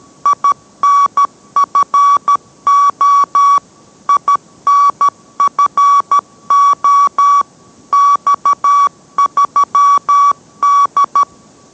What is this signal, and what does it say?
Morse code, heard as a keyed tone, about 13 wpm, 1140 Hz. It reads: INFO INFO X3D